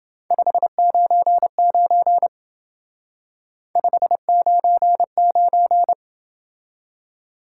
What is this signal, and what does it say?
Morse code, heard as a keyed tone, about 28 wpm, 700 Hz. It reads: 599 599